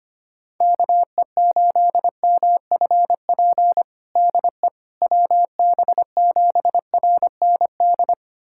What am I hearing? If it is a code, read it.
Morse code, heard as a keyed tone, about 25 wpm, 700 Hz. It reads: KE8MFP DE WB7RND
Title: USA Callsign DE Callsign - 6 Characters - 25wpm